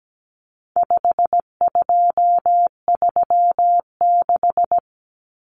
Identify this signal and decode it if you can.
Morse code, heard as a keyed tone, about 17 wpm, 700 Hz. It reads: 5236
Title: Numbers - 4-Digit 17wpm